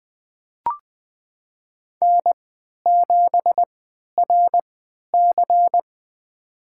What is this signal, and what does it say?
Morse code, heard as a keyed tone, about 20 wpm, spaced as for 11 wpm, 700 Hz. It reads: N7RC
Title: USA Call Signs - 3x Character Spacing - 20wpm